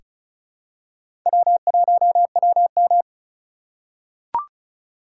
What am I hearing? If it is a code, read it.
Morse code, heard as a keyed tone, about 36 wpm, 700 Hz. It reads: W1WM
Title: USA Call Signs - 4 Characters - 2x Character Spacing - 35wpm